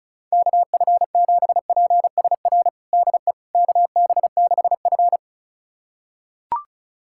Morse code, heard as a keyed tone, about 35 wpm, 700 Hz. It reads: KF7PSR DE KB6F
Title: USA Callsign DE Callsign 35wpm